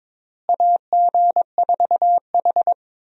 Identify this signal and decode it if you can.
Morse code, heard as a keyed tone, about 22 wpm, 700 Hz. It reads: AG4H
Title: USA Call Signs - 4 Characters - Repeated 2x - 22wpm